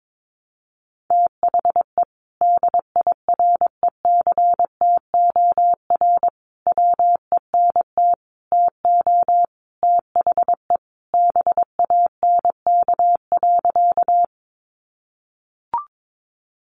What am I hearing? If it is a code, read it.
Morse code, heard as a keyed tone, about 22 wpm, 700 Hz. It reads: THE DIRECTOR WENT TO THE BANK.
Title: Sentences from Top 1300 Words 22wpm